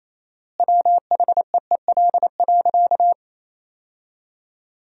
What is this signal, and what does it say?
Morse code, heard as a keyed tone, about 28 wpm, 700 Hz. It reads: WHEEL.
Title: 4x Word-Spacing - Sentences from Top 500 Words - 28wpm